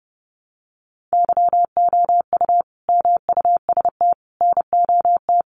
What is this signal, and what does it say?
Morse code, heard as a keyed tone, about 30 wpm, 700 Hz. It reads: YOU MUST NOT
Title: Sentences from Top 1200 Words 30wpm